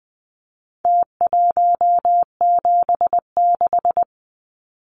Morse code, heard as a keyed tone, about 20 wpm, 700 Hz. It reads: T176
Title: Numbers - Contest Style - 4-Digit 20wpm